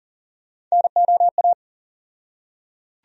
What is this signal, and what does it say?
Morse code, heard as a keyed tone, about 40 wpm, 700 Hz. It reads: NOA
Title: USA Call Sign Postfixes - 3 Letters 40wpm